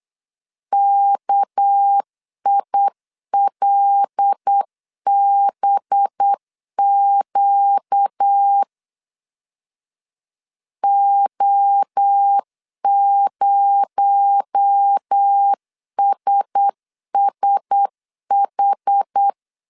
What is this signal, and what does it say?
Morse code, heard as a keyed tone, about 8 wpm, 800 Hz. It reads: KILBQ O0SSH
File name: CW / DET / 1 6 1500C S C 8wpm 5219